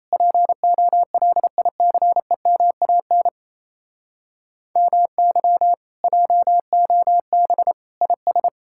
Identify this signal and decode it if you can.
Morse code, heard as a keyed tone, about 38 wpm, 700 Hz. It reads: POLICEMAN MY JOB IS